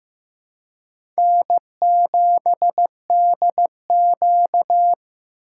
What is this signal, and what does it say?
Morse code, heard as a keyed tone, about 15 wpm, 700 Hz. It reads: N7DQ
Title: USA Call Signs - 4 Characters - 2x Character Spacing - 15wpm